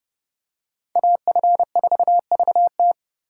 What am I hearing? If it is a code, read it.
Morse code, heard as a keyed tone, about 29 wpm, 700 Hz. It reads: AF4VT